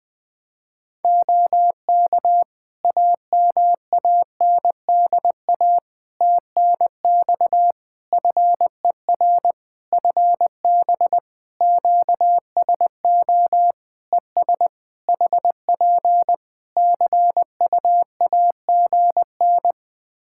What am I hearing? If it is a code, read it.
Morse code, heard as a keyed tone, about 20 wpm, 700 Hz. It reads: OK AMANDA TNX FER FB QSO ES HP CUAGN
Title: Practice QSO - Exchange 4 - Ending - 20wpm